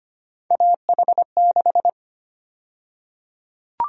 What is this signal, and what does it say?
Morse code, heard as a keyed tone, about 25 wpm, 700 Hz. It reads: AH6
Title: USA Call Sign Prefixes - Repeated 2x - 25wpm